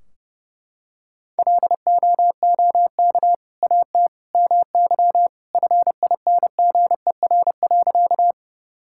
Morse code, heard as a keyed tone, about 30 wpm, 700 Hz. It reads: LOOK AT MY FINGER.
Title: Sentences from Top 1600 Words 30wpm